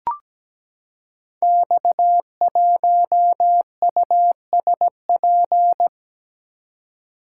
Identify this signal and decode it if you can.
Morse code, heard as a keyed tone, about 17 wpm, 700 Hz. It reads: X1USP
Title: Code Groups - 5 Letter:Numbers - 17wpm